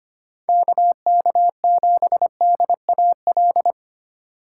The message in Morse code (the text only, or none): KK7DAL